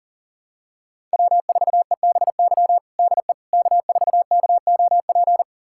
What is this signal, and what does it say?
Morse code, heard as a keyed tone, about 40 wpm, 700 Hz. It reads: W4EBY DE K4KOP
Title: USA Callsign DE Callsign - 5 Characters - 40wpm